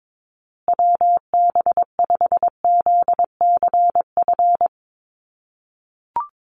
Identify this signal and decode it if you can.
Morse code, heard as a keyed tone, about 22 wpm, 700 Hz. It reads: WB5ZCF
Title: USA Call Signs Repeated 3x 22wpm